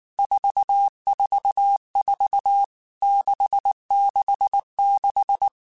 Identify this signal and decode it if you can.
Morse code, heard as a keyed tone, about 19 wpm, 800 Hz. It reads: 444 666
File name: Basic Letter Groups By7Groups / VB46 / VB46@18